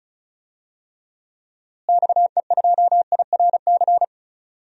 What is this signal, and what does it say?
Morse code, heard as a keyed tone, about 36 wpm, 700 Hz. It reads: XE2IRC
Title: DXCC-100 Call Signs - 6 Characters - Repeated 3x - 35wpm